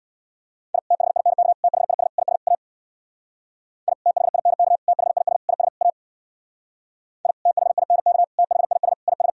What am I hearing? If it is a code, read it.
Morse code, heard as a keyed tone, about 65 wpm, 700 Hz. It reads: I THINK THIS IS A I THINK THIS IS A I THINK THIS IS